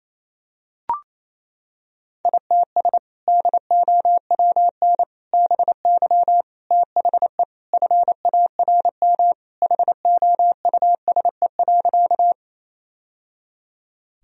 Morse code, heard as a keyed tone, about 27 wpm, 700 Hz. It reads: ITS DOWN BY THE FARM HOUSE.
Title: Sentences from Top 1000 Words - Review Part 2 - 28wpm